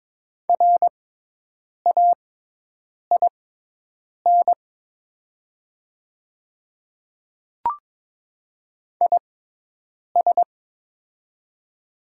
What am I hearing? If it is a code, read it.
Morse code, heard as a keyed tone, about 22 wpm, spaced as for 8 wpm, 700 Hz. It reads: RAIN IS
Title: 6x Character Spacing - Top 500 Words - 22wpm